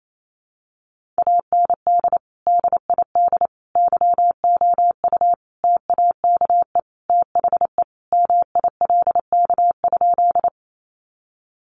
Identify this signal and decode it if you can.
Morse code, heard as a keyed tone, about 28 wpm, 700 Hz. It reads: AND DID YOU TAKE THE MILK?
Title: Sentences from Top 1500 Words 28wpm